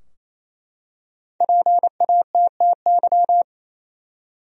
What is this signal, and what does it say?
Morse code, heard as a keyed tone, about 27 wpm, 700 Hz. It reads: PATTY
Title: USA Common Names - Max 5 Letters - 28wpm